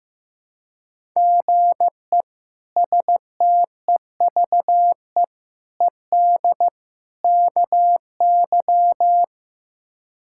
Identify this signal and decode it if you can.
Morse code, heard as a keyed tone, about 15 wpm, 700 Hz. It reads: GE STEVE ED KY
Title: K1USN SST QSO Practice - 15wpm